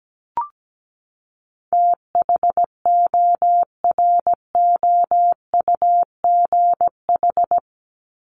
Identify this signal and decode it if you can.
Morse code, heard as a keyed tone, about 17 wpm, 700 Hz. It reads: THOROUGH
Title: LY Suffix - Top 100 Words with Base Words 17wpm